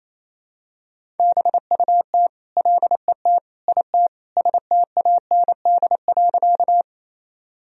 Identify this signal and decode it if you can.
Morse code, heard as a keyed tone, about 28 wpm, 700 Hz. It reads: BUT LET IT STAND.
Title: Sentences from Top 1000 Words - Review Part 1 - 28wpm